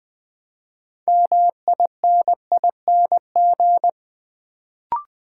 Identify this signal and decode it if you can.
Morse code, heard as a keyed tone, about 20 wpm, 700 Hz. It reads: MINING
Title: Build A Word - 3 Letters At A Time - 6-Letter Words - 20wpm